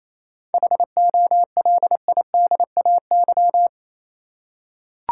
Morse code, heard as a keyed tone, about 28 wpm, 700 Hz. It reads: HOLIDAY
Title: Seven Letter Words - 28wpm